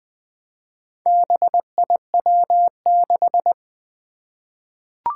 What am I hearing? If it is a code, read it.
Morse code, heard as a keyed tone, about 20 wpm, 700 Hz. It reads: BIW6
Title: Four Challenging Letter-Number Combinations 20wpm